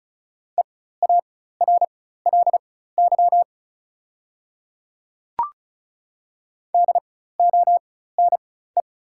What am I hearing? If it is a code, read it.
Morse code, heard as a keyed tone, about 36 wpm, spaced as for 16 wpm, 700 Hz. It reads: EARLY DONE
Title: 4x Character Spacing - Top 500 Words - 35wpm